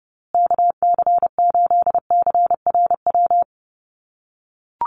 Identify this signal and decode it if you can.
Morse code, heard as a keyed tone, about 30 wpm, 700 Hz. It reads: KC8CRW
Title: USA Call Signs Repeated 2x 30wpm